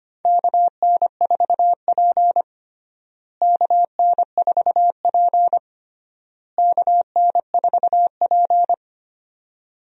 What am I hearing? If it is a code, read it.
Morse code, heard as a keyed tone, about 25 wpm, 700 Hz. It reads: KN4P KN4P KN4P
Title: USA Call Signs - 4 Characters - Repeated 3x - 25wpm